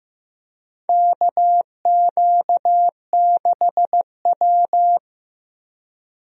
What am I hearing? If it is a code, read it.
Morse code, heard as a keyed tone, about 15 wpm, 700 Hz. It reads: KQ6W
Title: USA Call Signs - 4 Characters - 15wpm